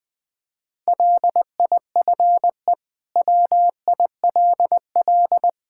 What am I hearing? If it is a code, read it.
Morse code, heard as a keyed tone, about 21 wpm, 700 Hz. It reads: LIFE WILL
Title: Sentences from Top 1000 Words - Review Part 5 - 20wpm